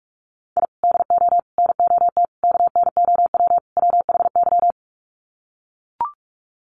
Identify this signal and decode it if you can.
Morse code, heard as a keyed tone, about 44 wpm, 700 Hz. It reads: I DO NOT KNOW WHY
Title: Sets of 5 Words 45wpm